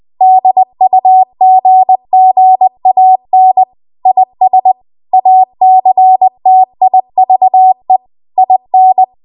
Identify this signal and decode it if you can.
Morse code, heard as a keyed tone, about 20 wpm, 750 Hz. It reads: DUGGAN IS ACTIVE IN
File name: Archive / 20 WPM / 20WPM